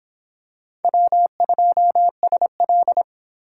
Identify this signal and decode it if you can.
Morse code, heard as a keyed tone, about 23 wpm, 700 Hz. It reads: W2SL